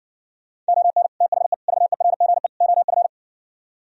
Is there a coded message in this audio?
Yes: Morse code from 0.7 to 3.1 s, heard as a keyed tone, about 59 wpm, 700 Hz. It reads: ON THE VERGE OF